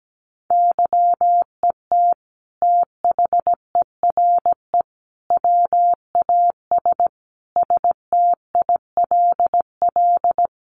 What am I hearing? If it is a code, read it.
Morse code, heard as a keyed tone, about 17 wpm, 700 Hz. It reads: YET THERE WAS STILL
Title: Sentences from Top 1300 Words 17wpm